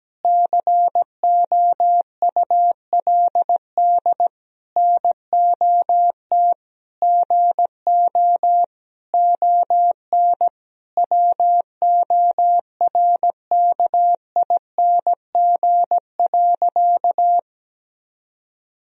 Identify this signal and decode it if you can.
Morse code, heard as a keyed tone, about 17 wpm, 700 Hz. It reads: COULD NOT GO ON WORKING.
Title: Sentences from Top 1000 Words - Review Part 4 - 17wpm